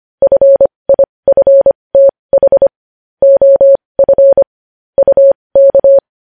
Morse code, heard as a keyed tone, about 25 wpm, spaced as for 20 wpm, 550 Hz. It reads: FIFTH OF UK
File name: podcast / News headlines-1x-25x20